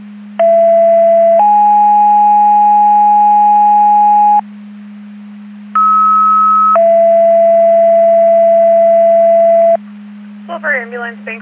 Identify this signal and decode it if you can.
Two-tone page, heard as a one-second tone, then a three-second tone, about 670 then 830 Hz, about 1290 then 670 Hz, with speech. It is heard from 0.4 to 9.8 s.